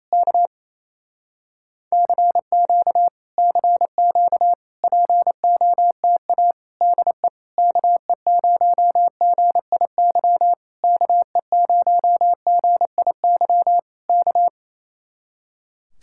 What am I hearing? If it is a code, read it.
Morse code, heard as a keyed tone, about 28 wpm, 700 Hz. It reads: K CQ CQ POTA DE KE0GIY KE0GIY K
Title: Speed-Racing Common QSO Phrases Encore - 28wpm